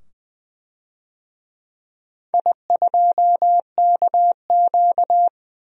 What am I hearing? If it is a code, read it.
Morse code, heard as a keyed tone, about 20 wpm, 700 Hz. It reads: I2KQ
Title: DXCC-100 Call Signs - 4 Characters - 20wpm